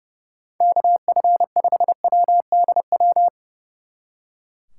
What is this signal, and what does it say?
Morse code, heard as a keyed tone, about 36 wpm, 700 Hz. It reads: KF5WDW
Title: Speed Racing - USA 2x3 Call Signs 30wpm